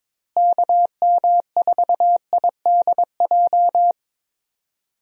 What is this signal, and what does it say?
Morse code, heard as a keyed tone, about 22 wpm, 700 Hz. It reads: KM4IDJ
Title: USA Call Signs - 6 Characters - Repeated 3x - 22wpm